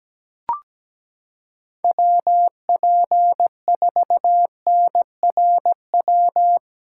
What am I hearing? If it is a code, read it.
Morse code, heard as a keyed tone, about 17 wpm, 700 Hz. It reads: WP4NRW
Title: USA Callsign DE Callsign - 6 Characters - 17wpm